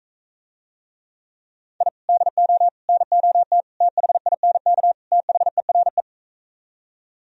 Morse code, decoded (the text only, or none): I DO NOT THINK THERE